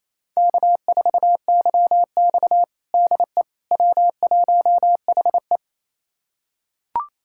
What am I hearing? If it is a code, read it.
Morse code, heard as a keyed tone, about 28 wpm, 700 Hz. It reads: K4YX DE W1HE
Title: USA Callsign DE Callsign - 4 Characters - 28wpm